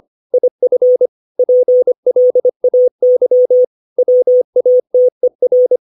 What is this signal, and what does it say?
Morse code, heard as a keyed tone, about 25 wpm, 500 Hz. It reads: IF PLAY WATER
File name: 632 words / 632 Words 25WPM